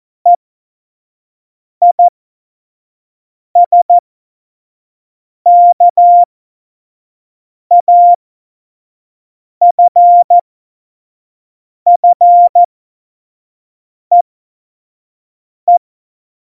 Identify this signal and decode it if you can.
Morse code, heard as a keyed tone, about 14 wpm, spaced as for 5 wpm, 700 Hz. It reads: EISKAFFEE
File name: Morsekurs / Witze / Witz 13. August 2019, 25 F70